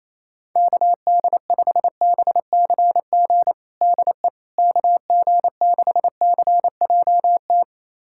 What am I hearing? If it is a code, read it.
Morse code, heard as a keyed tone, about 28 wpm, 700 Hz. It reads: KD5BCG DE KG6CJT
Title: USA Callsign DE Callsign 28wpm